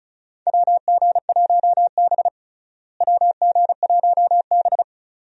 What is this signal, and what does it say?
Morse code, heard as a keyed tone, about 34 wpm, 700 Hz. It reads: WG1B WG1B
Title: USA Call Signs - 4 Characters - Repeated 2x - 35wpm